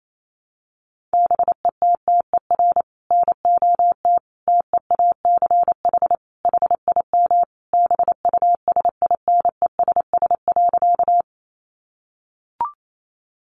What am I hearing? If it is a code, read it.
Morse code, heard as a keyed tone, about 28 wpm, 700 Hz. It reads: BETTER NOT TEACH HIM BUSINESS.
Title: Sentences from Top 600 Words 28wpm